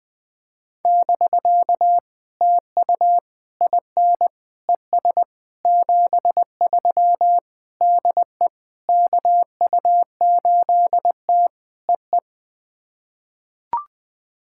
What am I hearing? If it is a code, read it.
Morse code, heard as a keyed tone, about 20 wpm, 700 Hz. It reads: <BK> TU IN ES 73 DE KU8T EE
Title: POTA QSO Practice - 20wpm